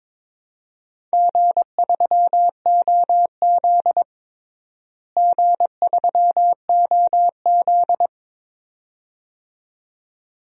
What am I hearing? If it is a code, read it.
Morse code, heard as a keyed tone, about 22 wpm, 700 Hz. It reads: G3OZ G3OZ
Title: DXCC-100 Call Signs - 4 Characters - Repeated 2x - 22wpm